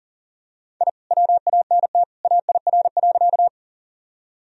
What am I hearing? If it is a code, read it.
Morse code, heard as a keyed tone, about 37 wpm, 700 Hz. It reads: I WANT AIR.
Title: Sentences from Top 1000 Words - Review Part 1 - 40wpm